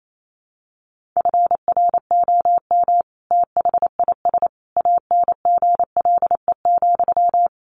Morse code, heard as a keyed tone, about 28 wpm, 700 Hz. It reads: FROM THIS ANGLE,
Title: Sentences from Top 1500 Words 28wpm